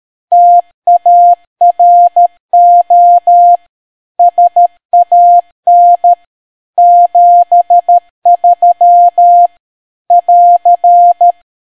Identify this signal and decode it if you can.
Morse code, heard as a keyed tone, about 13 wpm, 700 Hz. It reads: TARO SAN 73 <AR>